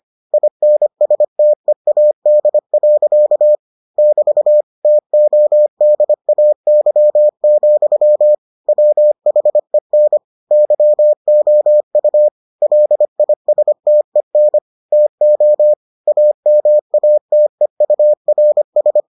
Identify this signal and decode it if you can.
Morse code, heard as a keyed tone, about 25 wpm, 600 Hz. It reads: INSTEAD. <BT> TODAY, WHEN YOU LISTEN TO AMATEURS